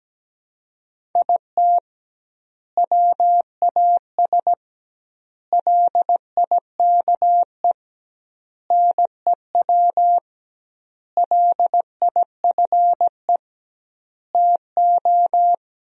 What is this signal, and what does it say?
Morse code, heard as a keyed tone, about 17 wpm, 700 Hz. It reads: IT WAS LIKE NEW LIFE TO
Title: 2x Word-Spacing - Sentences from Top 300 Words - 17wpm